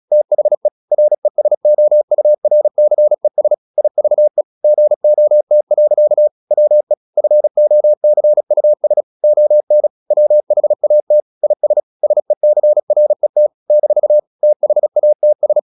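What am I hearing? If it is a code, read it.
Morse code, heard as a keyed tone, about 36 wpm, 600 Hz. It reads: THE RESOURCES IVE GOT. WE FOCUS ON WHAT IS SECRET - THATS
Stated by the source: Morse practice at 36 wpm, text of Ken Follett's novel Never, from CW med Gnister.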